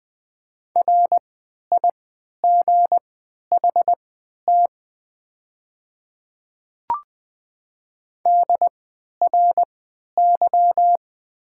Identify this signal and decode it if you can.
Morse code, heard as a keyed tone, about 20 wpm, spaced as for 11 wpm, 700 Hz. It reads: RIGHT DRY
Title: 3x Character Spacing - Top 500 Words - 20wpm